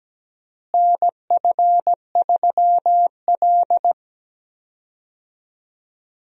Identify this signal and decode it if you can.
Morse code, heard as a keyed tone, about 17 wpm, 700 Hz. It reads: NF3L
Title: USA Call Signs - 4 Characters - Repeated 2x - 17wpm